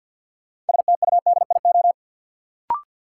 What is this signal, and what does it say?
Morse code, heard as a keyed tone, about 53 wpm, 700 Hz. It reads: STUDIO
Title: Sentences from Top 1200 Words - New Word Drill - 50wpm